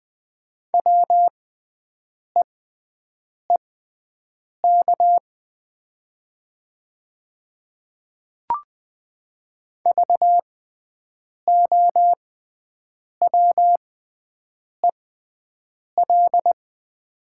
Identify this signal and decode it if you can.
Morse code, heard as a keyed tone, about 20 wpm, spaced as for 7 wpm, 700 Hz. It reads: WEEK VOWEL
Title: 6x Character Spacing - Top 500 Words - 20wpm